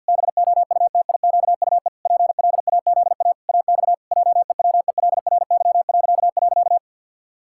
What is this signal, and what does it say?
Morse code, heard as a keyed tone, about 51 wpm, 700 Hz. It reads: BOUTIQUE PLAZA AX JEWELRY..
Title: Pangram Challenge v2 - 50wpm